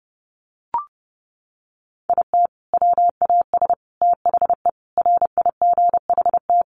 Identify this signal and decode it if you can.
Morse code, heard as a keyed tone, about 30 wpm, 700 Hz. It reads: IT WAS THE RIGHT
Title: Sentences from Top 1400 Words 30wpm